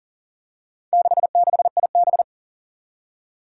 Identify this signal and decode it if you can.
Morse code, heard as a keyed tone, about 40 wpm, 700 Hz. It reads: 66IB
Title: Four Challenging Letter-Number Combinations 40wpm